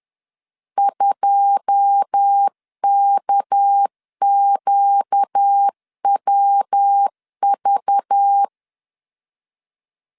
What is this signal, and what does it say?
Morse code, heard as a keyed tone, about 10 wpm, 800 Hz. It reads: 2KQWV